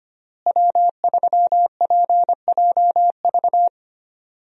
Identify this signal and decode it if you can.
Morse code, heard as a keyed tone, about 25 wpm, 700 Hz. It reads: W3PJV
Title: USA Call Signs - 5 Characters - Repeated 3x - 25wpm